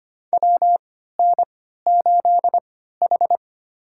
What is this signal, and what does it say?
Morse code, heard as a keyed tone, about 25 wpm, spaced as for 14 wpm, 700 Hz. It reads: WN8H
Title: USA Call Signs - 4 Characters - 3x Character Spacing - 25wpm